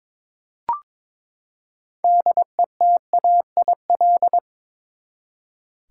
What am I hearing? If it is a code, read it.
Morse code, heard as a keyed tone, about 22 wpm, 700 Hz. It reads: DETAIL
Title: Six Letter Words - 22wpm